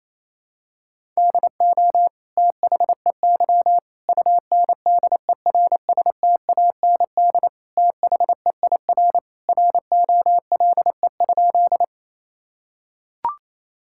Morse code, heard as a keyed tone, about 28 wpm, 700 Hz. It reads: DO THEY UNDERSTAND THEIR ROLE?
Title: Sentences from Top 1000 Words - Review Part 6 - 28wpm